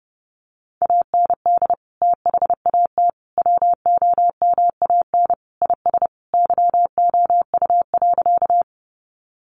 Morse code, heard as a keyed tone, about 30 wpm, 700 Hz. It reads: AND THAT WOMAN IS YOU.
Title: Sentences from Top 1100 Words 30wpm